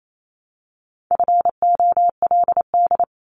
Morse code, heard as a keyed tone, about 28 wpm, 700 Hz. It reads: FOLD
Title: Multi-letter Phonemes 28wpm